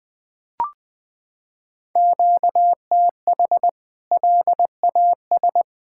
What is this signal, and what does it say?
Morse code, heard as a keyed tone, about 20 wpm, 700 Hz. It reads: QTH LAS
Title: K1USN SST QSO Practice 20wpm